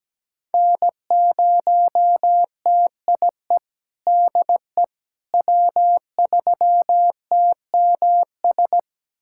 Morse code, heard as a keyed tone, about 17 wpm, 700 Hz. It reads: N0TIE DE W3TMS
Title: USA Callsign DE Callsign - 5 Characters - 17wpm